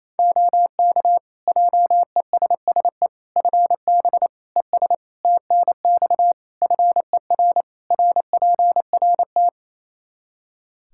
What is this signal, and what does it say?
Morse code, heard as a keyed tone, about 27 wpm, 700 Hz. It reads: OK JESSE FB ES TNX FER RPRT
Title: Practice QSO - Exchange 2 - Info - 28wpm